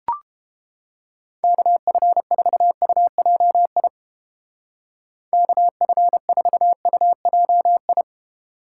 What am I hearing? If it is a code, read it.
Morse code, heard as a keyed tone, about 29 wpm, 700 Hz. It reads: KF4UJI KF4UJI